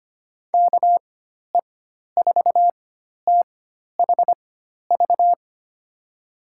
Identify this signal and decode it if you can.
Morse code, heard as a keyed tone, about 25 wpm, spaced as for 12 wpm, 700 Hz. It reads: KE4THV
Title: USA Call Signs - 6 Characters - 4x Character Spacing - 25wpm